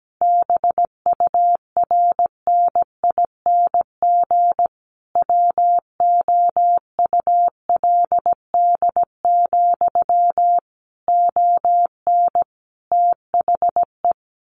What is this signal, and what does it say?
Morse code, heard as a keyed tone, about 17 wpm, 700 Hz. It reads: BURNING WOULD, ON THE
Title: A Sound Of Thunder 17wpm